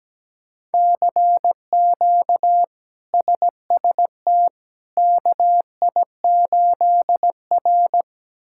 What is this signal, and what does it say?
Morse code, heard as a keyed tone, about 17 wpm, 700 Hz. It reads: CQ SST KI8R
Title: K1USN SST QSO Practice - 17wpm